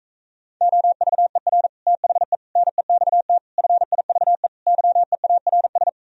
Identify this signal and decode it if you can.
Morse code, heard as a keyed tone, about 45 wpm, 700 Hz. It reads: OVER THE NEXT FIVE YEARS